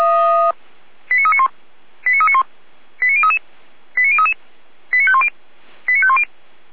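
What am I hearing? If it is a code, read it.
Five-tone selective call (ZVEI), heard as quick runs of five tones, tones of about 70 ms. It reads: 89391 89391 89033 89033 89510 89510
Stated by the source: ZVEI five-tone selective call received via SDR.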